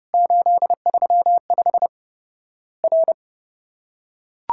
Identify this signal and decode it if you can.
Morse code, heard as a keyed tone, about 30 wpm, 700 Hz. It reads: 835 R
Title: Worked All Europe DX Contest - QTC Exchange - 30wpm